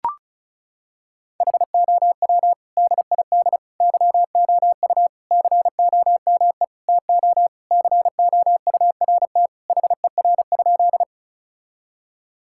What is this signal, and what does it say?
Morse code, heard as a keyed tone, about 34 wpm, 700 Hz. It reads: HOW DID YOU COME TO COURT HER?
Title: Sentences from Top 1600 Words 35wpm